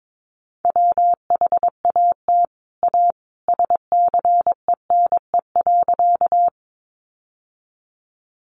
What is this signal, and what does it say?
Morse code, heard as a keyed tone, about 22 wpm, 700 Hz. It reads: WHAT A SCENE.
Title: Sentences from Top 1100 Words 22wpm